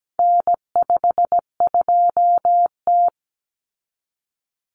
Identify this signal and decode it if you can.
Morse code, heard as a keyed tone, about 17 wpm, 700 Hz. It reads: N52T
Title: Common Cut Numbers - 4-Digit 17wpm